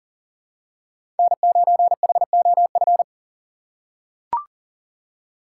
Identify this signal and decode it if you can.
Morse code, heard as a keyed tone, about 40 wpm, 700 Hz. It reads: N9HOF
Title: USA Call Signs - 5 Characters - 2x Character Spacing - 40wpm